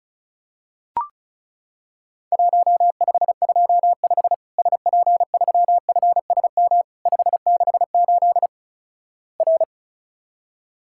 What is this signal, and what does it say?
Morse code, heard as a keyed tone, about 34 wpm, 700 Hz. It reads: 1525 SP3FSM 568 R
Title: Worked All Europe DX Contest - QTC Exchange - 35wpm